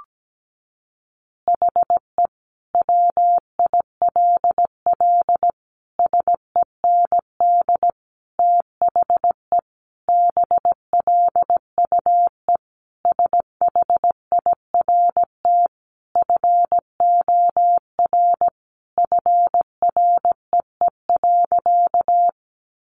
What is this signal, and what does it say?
Morse code, heard as a keyed tone, about 17 wpm, 700 Hz. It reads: HE WILL SEND THE BLUE SHIRT FOR FREE.
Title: Additional Easy Phrases 17wpm